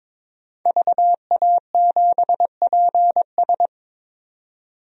Morse code, heard as a keyed tone, about 22 wpm, 700 Hz. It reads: VA7PS
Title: DXCC-100 Call Signs - 5 Characters - 22wpm